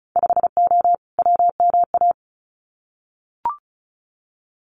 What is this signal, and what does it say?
Morse code, heard as a keyed tone, about 36 wpm, 690 Hz. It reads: 5O WMA
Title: Winter Field Day Exchanges 35wpm